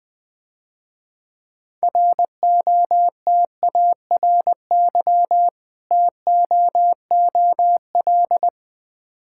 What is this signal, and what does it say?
Morse code, heard as a keyed tone, about 21 wpm, 700 Hz. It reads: ROTARY TOOL
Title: Difficult Words - Speed-Racing with Hints - 20wpm